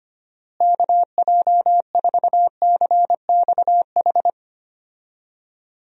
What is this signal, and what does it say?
Morse code, heard as a keyed tone, about 25 wpm, 700 Hz. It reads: KJ4CXH
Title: USA Call Signs - 6 Characters - Repeated 3x - 25wpm